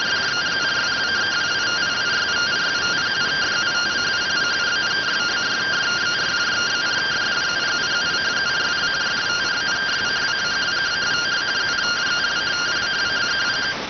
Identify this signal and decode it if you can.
Two-tone teleprinter signal, about 170 Hz shift.